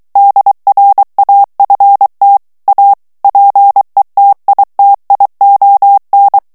Morse code, heard as a keyed tone, about 23 wpm, 800 Hz. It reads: DRAFT A PETITION
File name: ANGLAIS texte clair 3 vitesse 22 mots minute